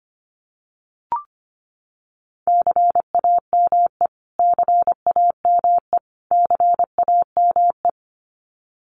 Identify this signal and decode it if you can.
Morse code, heard as a keyed tone, about 25 wpm, 700 Hz. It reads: CAME CAME CAME
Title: Top 500 Words Repeated 25wpm